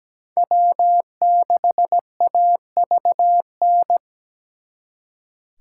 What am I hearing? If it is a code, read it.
Morse code, heard as a keyed tone, about 20 wpm, 700 Hz. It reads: W6AVN
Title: Speed Racing - USA 1x3 Call Signs 17wpm